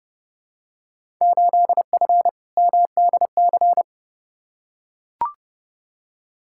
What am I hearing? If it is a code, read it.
Morse code, heard as a keyed tone, about 31 wpm, 700 Hz. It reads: 8F MDC
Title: Field Day Exchanges Encore - 30wpm